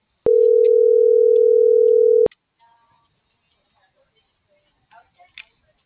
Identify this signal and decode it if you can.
Ringback tone (North America), 440 + 480 Hz.